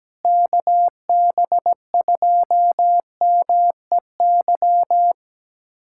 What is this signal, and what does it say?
Morse code, heard as a keyed tone, about 17 wpm, 700 Hz. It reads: KB2MEY
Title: USA Callsign DE Callsign - 6 Characters - 17wpm